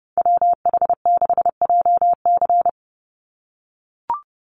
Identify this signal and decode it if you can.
Morse code, heard as a keyed tone, about 30 wpm, 700 Hz. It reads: WH6JC
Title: USA Call Signs 30wpm